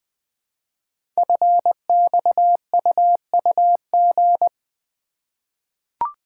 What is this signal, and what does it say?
Morse code, heard as a keyed tone, about 21 wpm, 700 Hz. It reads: FXUUG
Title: Code Groups - 5 Letters - 20wpm